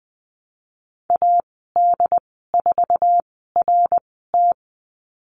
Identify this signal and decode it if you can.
Morse code, heard as a keyed tone, about 20 wpm, spaced as for 15 wpm, 700 Hz. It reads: AD4RT
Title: 082 Lesson 14g - Number 2 - Call Signs 20wpm